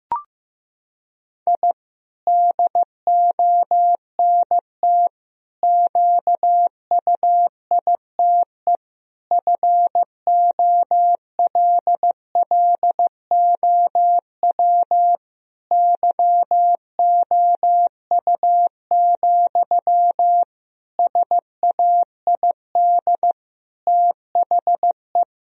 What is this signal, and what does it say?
Morse code, heard as a keyed tone, about 15 wpm, 700 Hz. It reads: I DONT QUITE FOLLOW YOU, SAID THE
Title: Sentences from Top 1000 Words - Review Part 4 - 15wpm